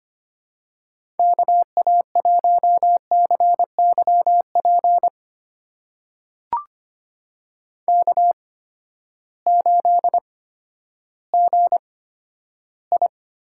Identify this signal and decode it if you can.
Morse code, heard as a keyed tone, about 25 wpm, spaced as for 12 wpm, 700 Hz. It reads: KA1CYP K8GI
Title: USA Call Signs - 8x Character Spacing - 25wpm